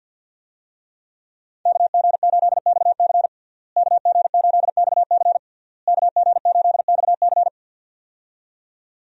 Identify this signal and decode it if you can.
Morse code, heard as a keyed tone, about 48 wpm, 700 Hz. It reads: KG8XC KG8XC KG8XC
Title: USA Call Signs - 5 Characters - Repeated 3x - 50wpm